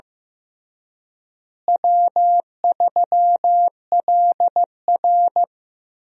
Morse code, heard as a keyed tone, about 16 wpm, 700 Hz. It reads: W3LR
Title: USA Call Signs - 4 Characters - Repeated 3x - 15wpm